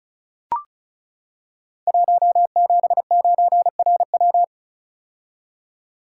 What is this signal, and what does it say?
Morse code, heard as a keyed tone, about 36 wpm, 700 Hz. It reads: 179RW
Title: Code Groups - 5 Letter:Numbers - 35wpm